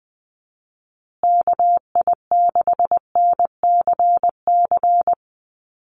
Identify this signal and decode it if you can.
Morse code, heard as a keyed tone, about 20 wpm, 700 Hz. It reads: KI6NCC
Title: USA Call Signs Repeated 3x 20wpm